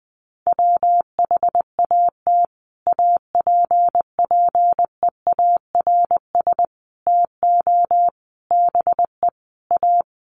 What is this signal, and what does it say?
Morse code, heard as a keyed tone, about 20 wpm, 700 Hz. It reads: WHAT APPEARS TO BE A
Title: Sets of 5 Words Encore2 20wpm